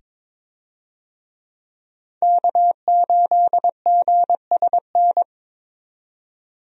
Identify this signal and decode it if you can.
Morse code, heard as a keyed tone, about 22 wpm, 700 Hz. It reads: K8GSN
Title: USA Call Signs - 5 Characters - 22wpm